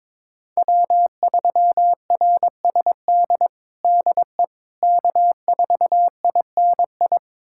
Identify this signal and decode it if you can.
Morse code, heard as a keyed tone, about 22 wpm, 700 Hz. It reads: W3RSD DE K4INI
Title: USA Callsign DE Callsign - 5 Characters - 22wpm